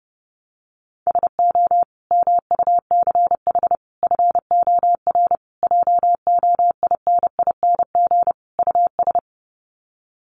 Morse code, heard as a keyed tone, about 30 wpm, 700 Hz. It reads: SO MUCH FOR JOINING US
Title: Sets of 5 Words 30wpm